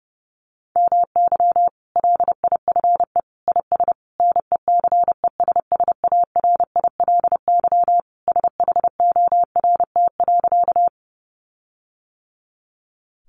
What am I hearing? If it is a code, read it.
Morse code, heard as a keyed tone, about 30 wpm, 700 Hz. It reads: MY LIFE IS NECESSARILY SHORT.
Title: Sentences from Top 1400 Words 30wpm